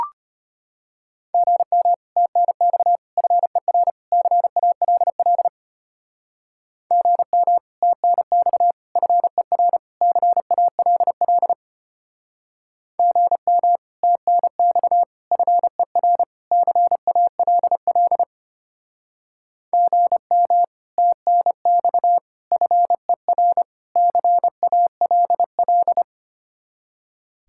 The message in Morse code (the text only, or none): GM TNX FER CALL GM TNX FER CALL GM TNX FER CALL GM TNX FER CALL